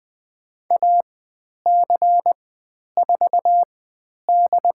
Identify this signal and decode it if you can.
Morse code, heard as a keyed tone, about 20 wpm, spaced as for 10 wpm, 700 Hz. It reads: AC4D
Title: 109 Lesson 18g - Letter m - Call Signs 20wpm